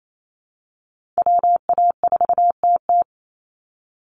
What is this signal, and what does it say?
Morse code, heard as a keyed tone, about 28 wpm, 700 Hz. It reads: WA4TT
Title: Call Signs Repeated 3x - 28wpm